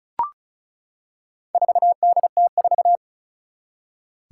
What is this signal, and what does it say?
Morse code, heard as a keyed tone, about 36 wpm, 700 Hz. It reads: 4DT4
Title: Four Challenging Letter-Number Combinations 35wpm